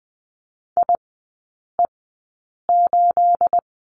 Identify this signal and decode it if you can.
Morse code, heard as a keyed tone, about 20 wpm, spaced as for 8 wpm, 690 Hz. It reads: IE8
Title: 212 Lesson 34e - Number 8 - Three Characters 20wpm